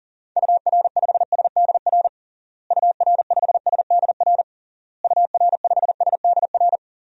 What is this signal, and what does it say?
Morse code, heard as a keyed tone, about 37 wpm, 700 Hz. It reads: UR5SDR UR5SDR UR5SDR
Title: DXCC-100 Call Signs - 6 Characters - Repeated 3x - 40wpm